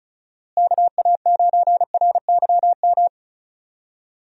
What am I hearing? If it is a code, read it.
Morse code, heard as a keyed tone, about 34 wpm, 700 Hz. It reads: KA9RYM
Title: USA Call Signs - 6 Characters - Repeated 3x - 35wpm